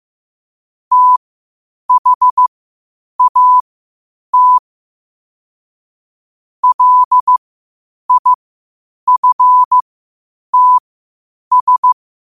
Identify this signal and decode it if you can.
Morse code, heard as a keyed tone, about 15 wpm, spaced as for 8 wpm, 1000 Hz. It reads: THAT LIFTS